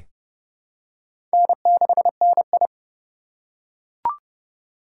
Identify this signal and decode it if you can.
Morse code, heard as a keyed tone, about 32 wpm, 700 Hz. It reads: N6NI
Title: USA Call Signs - 4x Character Spacing - 30wpm